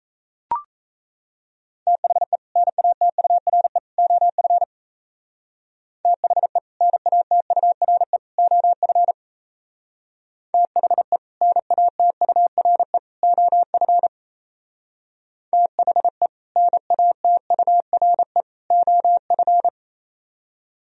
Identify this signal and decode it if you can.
Morse code, heard as a keyed tone, about 35 wpm, 700 Hz. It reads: THE NATURE OF THE NATURE OF THE NATURE OF THE NATURE OF